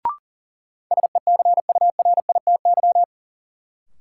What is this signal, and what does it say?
Morse code, heard as a keyed tone, about 40 wpm, 700 Hz. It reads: SECURITY
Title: Sentences from Top 1400 Words - New Word Drill Alt - 40wpm